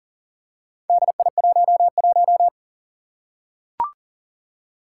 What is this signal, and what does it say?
Morse code, heard as a keyed tone, about 40 wpm, 700 Hz. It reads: DI11
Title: Four Challenging Letter-Number Combinations 40wpm